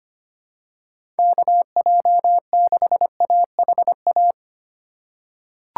Morse code, heard as a keyed tone, about 25 wpm, 700 Hz. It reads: KJ6AHA
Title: USA Call Signs - 6 Characters - 2x Character Spacing - 25wpm